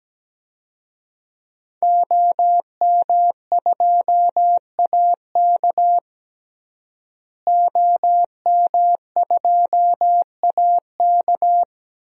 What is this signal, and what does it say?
Morse code, heard as a keyed tone, about 17 wpm, 700 Hz. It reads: OM2AK OM2AK
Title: DXCC-100 Call Signs - 5 Characters - Repeated 2x - 17wpm